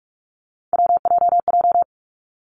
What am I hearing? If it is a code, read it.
Morse code, heard as a keyed tone, about 46 wpm, 700 Hz. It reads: WJJ
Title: Three Challenging Letter-Number Combinations 45wpm